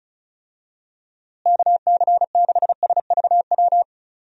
Morse code, heard as a keyed tone, about 34 wpm, 700 Hz. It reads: KC6SVW
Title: USA Call Signs - 6 Characters - Repeated 2x - 35wpm